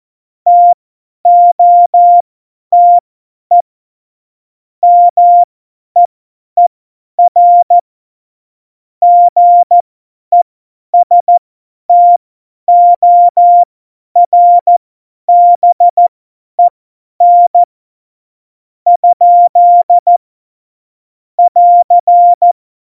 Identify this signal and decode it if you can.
Morse code, heard as a keyed tone, about 14 wpm, spaced as for 10 wpm, 700 Hz. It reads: TOTE MEER GESTORBEN ? <AR>
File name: Morsekurs / Witze / Witz 16. April 2019, 50 F70